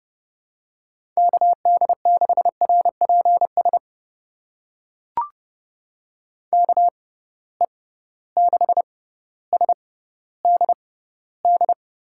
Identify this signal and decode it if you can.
Morse code, heard as a keyed tone, about 32 wpm, spaced as for 16 wpm, 700 Hz. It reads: KD6RPS KE6SDD
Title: USA Call Signs - 6 Characters - 6x Character Spacing - 30wpm